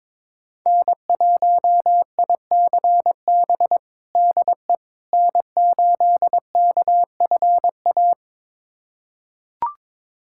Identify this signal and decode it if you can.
Morse code, heard as a keyed tone, about 22 wpm, 700 Hz. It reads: N1ICB DE N8KFA
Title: USA Callsign DE Callsign - 5 Characters - 22wpm